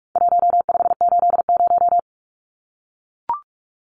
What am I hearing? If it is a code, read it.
Morse code, heard as a keyed tone, about 44 wpm, 700 Hz. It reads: 1580
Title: Numbers - Contest Style - 4-Digit 45wpm